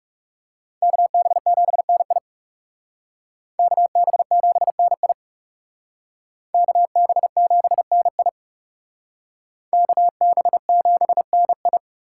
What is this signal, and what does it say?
Morse code, heard as a keyed tone, about 36 wpm, 700 Hz. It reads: KB7NI KB7NI KB7NI KB7NI
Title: Speed Racing - USA 2x2 Call Signs 30wpm